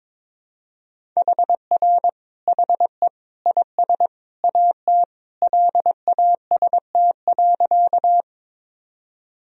Morse code, heard as a keyed tone, about 22 wpm, 700 Hz. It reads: HR HE IS AT LAST.
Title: Sentences from Top 300 Words and Abbreviations 22wpm